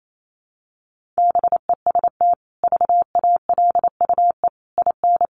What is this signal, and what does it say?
Morse code, heard as a keyed tone, about 28 wpm, 700 Hz. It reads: BEST VALUE IN
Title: Sentences from Top 600 Words 28wpm